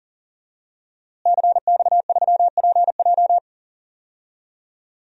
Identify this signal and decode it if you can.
Morse code, heard as a keyed tone, about 40 wpm, 700 Hz. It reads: CX3PJ
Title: DXCC-100 Call Signs - 5 Characters - Repeated 3x - 40wpm